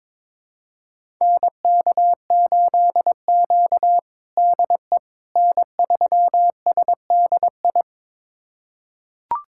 Morse code, heard as a keyed tone, about 22 wpm, 700 Hz. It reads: NK8Q DE N3SDI
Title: USA Callsign DE Callsign 22wpm